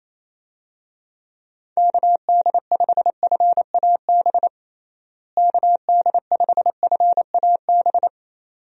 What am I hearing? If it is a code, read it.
Morse code, heard as a keyed tone, about 27 wpm, 700 Hz. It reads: KD5FAB KD5FAB
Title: USA Call Signs - 6 Characters - Repeated 2x - 28wpm